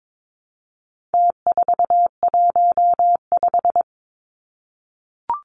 Morse code, heard as a keyed tone, about 22 wpm, 700 Hz. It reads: T415
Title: Numbers - Contest Style - 4-Digit 22wpm